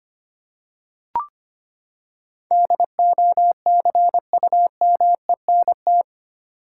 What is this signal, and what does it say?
Morse code, heard as a keyed tone, about 25 wpm, 700 Hz. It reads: DOCUMENT
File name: Eight Letter Words - 25wpm